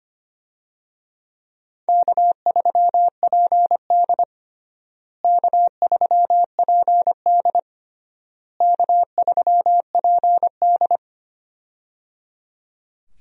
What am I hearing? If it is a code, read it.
Morse code, heard as a keyed tone, about 25 wpm, 700 Hz. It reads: K3PD K3PD K3PD
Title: USA Call Signs - 4 Characters - Repeated 3x - 25wpm